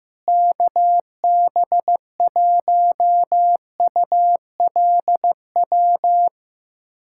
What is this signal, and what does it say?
Morse code, heard as a keyed tone, about 16 wpm, 700 Hz. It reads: KB1ULW
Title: USA Callsign DE Callsign 15wpm